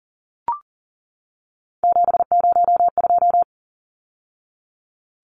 Morse code, heard as a keyed tone, about 40 wpm, 700 Hz. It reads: 702
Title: Numbers - 3-Digit 40wpm